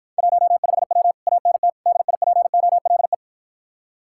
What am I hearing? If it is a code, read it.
Morse code, heard as a keyed tone, about 53 wpm, 700 Hz. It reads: 15W ANT DIPOLE